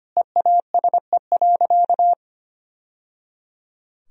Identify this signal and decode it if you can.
Morse code, heard as a keyed tone, about 25 wpm, 700 Hz. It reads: EASE.
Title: Sentences from Top 1000 Words - Review Part 1 - 25wpm